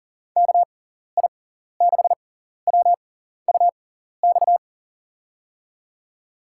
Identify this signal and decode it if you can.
Morse code, heard as a keyed tone, about 40 wpm, spaced as for 14 wpm, 700 Hz. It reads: KI6WUX
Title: USA Call Signs - 6 Characters - 6x Character Spacing - 40wpm